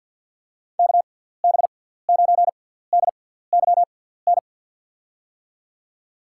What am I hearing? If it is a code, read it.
Morse code, heard as a keyed tone, about 47 wpm, spaced as for 17 wpm, 700 Hz. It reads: KB9DYN
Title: USA Call Signs - 6 Characters - 6x Character Spacing - 50wpm